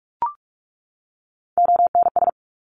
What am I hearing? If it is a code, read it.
Morse code, heard as a keyed tone, about 44 wpm, 700 Hz. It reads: ONS
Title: ARRL Sections 45wpm